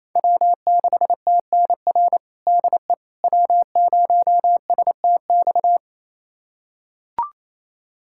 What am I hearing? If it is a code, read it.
Morse code, heard as a keyed tone, about 28 wpm, 700 Hz. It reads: W6TNR DE W0STX
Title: USA Callsign DE Callsign - 5 Characters - 28wpm